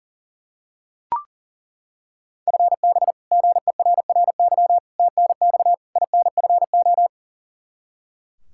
Morse code, heard as a keyed tone, about 37 wpm, 700 Hz. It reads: FB GERRY TNX INFO
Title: Practice QSO - Exchange 3 - More Info - 40wpm